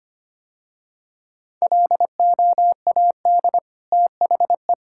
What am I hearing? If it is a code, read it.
Morse code, heard as a keyed tone, about 25 wpm, 700 Hz. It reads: LOAD THE
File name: Easy Sentences with Hints - 25wpm